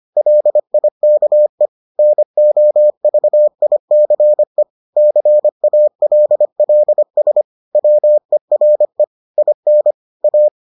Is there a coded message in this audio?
Yes: Morse code from 0.2 to 10.5 s, heard as a keyed tone, about 25 wpm, 600 Hz. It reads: LIKE NOVICE CALLS WERE IN A